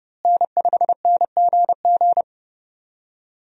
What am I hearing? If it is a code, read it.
Morse code, heard as a keyed tone, about 29 wpm, 700 Hz. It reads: N5NGG
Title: USA Call Signs - 5 Characters - Repeated 3x - 30wpm